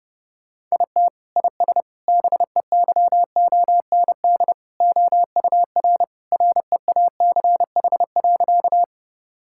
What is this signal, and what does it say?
Morse code, heard as a keyed tone, about 29 wpm, 700 Hz. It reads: IT IS BEYOND OUR REACH.
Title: Sentences from Top 1000 Words - Review Part 6 - 30wpm